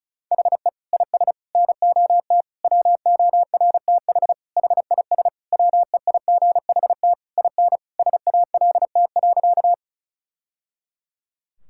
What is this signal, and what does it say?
Morse code, heard as a keyed tone, about 36 wpm, 700 Hz. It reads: HE IS NOT WORTH HIS WEIGHT IN SALT.
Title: Sentences from Top 1000 Words - Review Part 5 - 35wpm